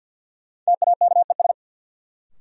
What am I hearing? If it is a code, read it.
Morse code, heard as a keyed tone, about 52 wpm, 700 Hz. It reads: TAKES
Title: Sets of 5 Words Encore - New Word Drill - 50wpm